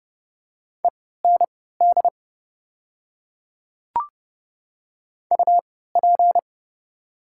Morse code, heard as a keyed tone, about 30 wpm, spaced as for 17 wpm, 690 Hz. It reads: END UP
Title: 3x Character Spacing - Top 200 Words - 30wpm